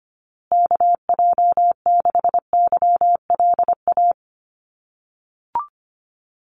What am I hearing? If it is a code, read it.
Morse code, heard as a keyed tone, about 25 wpm, 700 Hz. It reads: KJ6YLA
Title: USA Call Signs Repeated 2x 25wpm